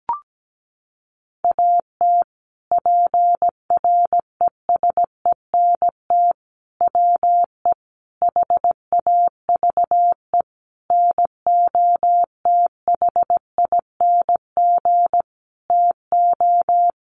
Morse code, heard as a keyed tone, about 17 wpm, 700 Hz. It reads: AT PRESENT WE HAVE NOTHING TO
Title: Sentences from Top 800 Words 17wpm